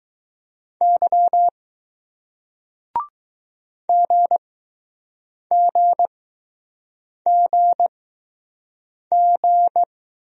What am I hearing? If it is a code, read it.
Morse code, heard as a keyed tone, about 20 wpm, spaced as for 6 wpm, 700 Hz. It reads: Y GGGG